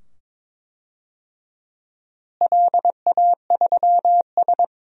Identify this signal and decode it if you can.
Morse code, heard as a keyed tone, about 22 wpm, 700 Hz. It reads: LA3S
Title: DXCC-100 Call Signs - 4 Characters - Repeated 2x - 22wpm